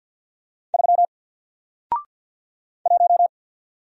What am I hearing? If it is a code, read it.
Morse code, heard as a keyed tone, about 48 wpm, 700 Hz. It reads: 3 1
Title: Single Character (Letter:Number:Punctuation:Prosign) Alt 50wpm